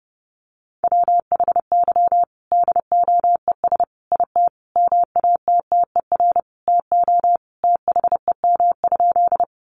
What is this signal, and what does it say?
Morse code, heard as a keyed tone, about 30 wpm, 700 Hz. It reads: WHY DOES IT MATTER TO THEM?
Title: Sentences from Top 700 Words 30wpm